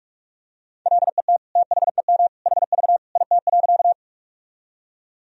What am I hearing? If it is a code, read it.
Morse code, heard as a keyed tone, about 46 wpm, 700 Hz. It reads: LET THEM HV IT.
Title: Sentences from Top 300 Words and Abbreviations 45wpm